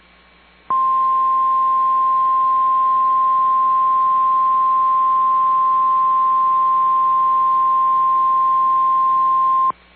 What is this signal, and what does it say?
Emergency Alert System (SAME) tones (the 1050 Hz attention tone). It is heard from 0.7 to 9.8 s.
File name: EAS Logs / KMNR